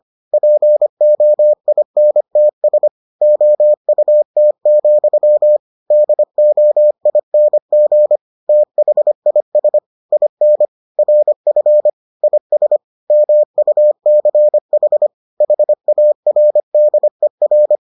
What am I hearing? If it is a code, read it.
Morse code, heard as a keyed tone, about 25 wpm, 600 Hz. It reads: POINTS OUT, DOING THIS IN RF IS MUCH HARDER